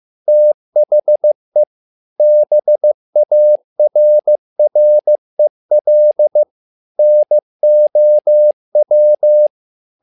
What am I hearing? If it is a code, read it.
Morse code, heard as a keyed tone, about 15 wpm, 600 Hz. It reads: THE BARREL NOW
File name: learning center / tower trouble / tower trouble 15wpm